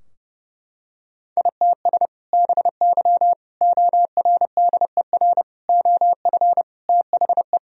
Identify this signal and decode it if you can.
Morse code, heard as a keyed tone, about 30 wpm, 700 Hz. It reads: ITS BY ORDER OF THE
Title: Sentences from Top 1000 Words - Review Part 3 - 30wpm